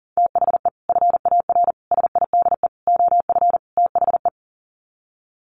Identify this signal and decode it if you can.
Morse code, heard as a keyed tone, about 40 wpm, 700 Hz. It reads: THE FAR SIDE OF THE
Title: Sets of 5 Words 40wpm